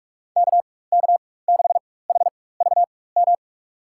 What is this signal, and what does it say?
Morse code, heard as a keyed tone, about 44 wpm, spaced as for 21 wpm, 700 Hz. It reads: KK6HVM
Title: USA Call Signs - 6 Characters - 4x Character Spacing - 45wpm